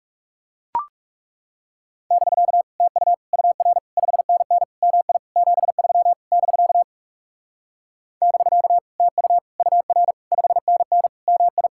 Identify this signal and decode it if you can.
Morse code, heard as a keyed tone, about 37 wpm, 700 Hz. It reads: <BK> TU UR 5NN MI 73 <BK> <BK> TU UR 5NN MI
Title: Speed-Racing Common QSO Phrases Encore - 30wpm